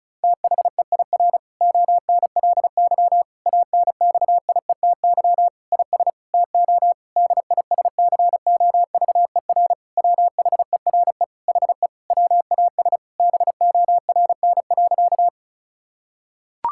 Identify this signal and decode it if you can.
Morse code, heard as a keyed tone, about 34 wpm, 700 Hz. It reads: THEIR ONLY ANXIETY IS TO DISCOVER WHERE HE WAS BORN.
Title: Sentences from Top 1600 Words 35wpm